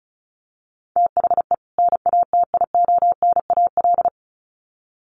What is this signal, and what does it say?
Morse code, heard as a keyed tone, about 34 wpm, 700 Hz. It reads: THE NATIONAL
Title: Sets of 2 Words 35wpm